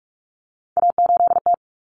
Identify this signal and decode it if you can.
Morse code, heard as a keyed tone, about 44 wpm, 700 Hz. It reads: A8T
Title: Common Cut Numbers - 3-Digit 45wpm